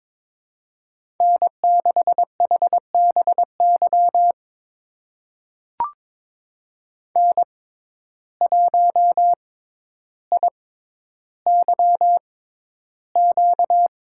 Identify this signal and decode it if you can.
Morse code, heard as a keyed tone, about 22 wpm, spaced as for 11 wpm, 700 Hz. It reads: N6HBY N1IYQ
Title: USA Call Signs - 6x Character Spacing - 22wpm